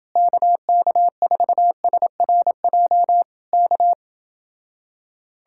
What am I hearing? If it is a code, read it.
Morse code, heard as a keyed tone, about 27 wpm, 700 Hz. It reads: KK4SRJ K